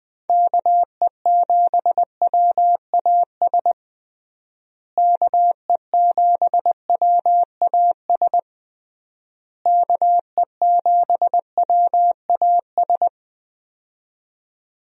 Morse code, heard as a keyed tone, about 20 wpm, 700 Hz. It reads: KE7WAS KE7WAS KE7WAS
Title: USA Call Signs - 6 Characters - Repeated 3x - 20wpm